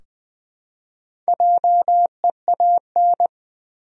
Morse code, heard as a keyed tone, about 20 wpm, 700 Hz. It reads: JEAN
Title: USA Common Names - 20wpm